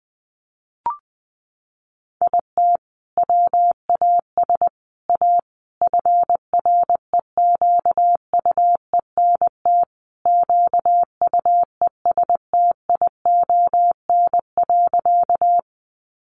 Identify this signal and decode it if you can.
Morse code, heard as a keyed tone, about 20 wpm, 700 Hz. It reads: IT WAS A FREQUENT QUESTION.
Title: Sentences from Top 1400 Words 20wpm